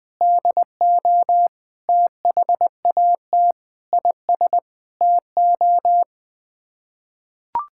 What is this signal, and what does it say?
Morse code, heard as a keyed tone, about 21 wpm, 700 Hz. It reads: DO THAT IS TO
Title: Sets of 9 Words - 20wpm